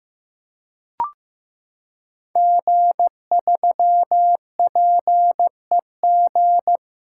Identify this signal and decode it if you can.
Morse code, heard as a keyed tone, about 15 wpm, 700 Hz. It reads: G3PEG
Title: Code Groups - 5 Letter:Numbers - 15wpm